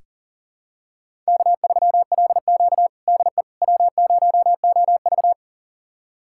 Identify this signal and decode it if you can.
Morse code, heard as a keyed tone, about 40 wpm, 700 Hz. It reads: K3LQ DE W0OV
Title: USA Callsign DE Callsign - 4 Characters - 40wpm